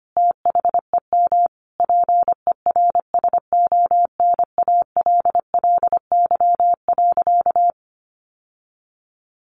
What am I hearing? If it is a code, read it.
Morse code, heard as a keyed tone, about 25 wpm, 700 Hz. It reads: THEM PERSONALLY.
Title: Sentences from Top 1400 Words 25wpm